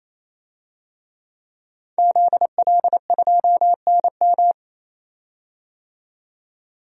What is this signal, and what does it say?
Morse code, heard as a keyed tone, about 28 wpm, 700 Hz. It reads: ZL2NM
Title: DXCC-100 Call Signs - 5 Characters - 28wpm